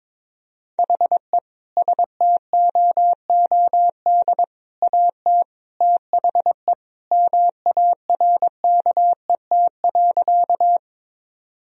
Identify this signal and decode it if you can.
Morse code, heard as a keyed tone, about 21 wpm, 700 Hz. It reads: HE STOOD AT THE MARKET.
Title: Sentences from Top 1000 Words - Review Part 3 - 22wpm